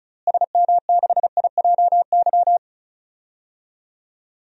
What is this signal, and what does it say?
Morse code, heard as a keyed tone, about 36 wpm, 700 Hz. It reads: SM6IJY
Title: DXCC-100 Call Signs - 6 Characters - Repeated 2x - 35wpm